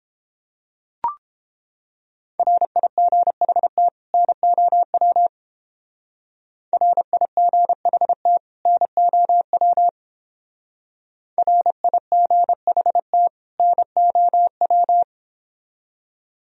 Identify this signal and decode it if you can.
Morse code, heard as a keyed tone, about 31 wpm, 700 Hz. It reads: RIGHT NOW RIGHT NOW RIGHT NOW